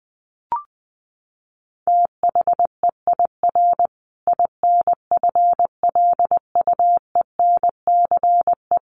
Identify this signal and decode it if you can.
Morse code, heard as a keyed tone, about 20 wpm, 700 Hz. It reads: THEIR INFLUENCE
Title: Sentences from Top 1400 Words 20wpm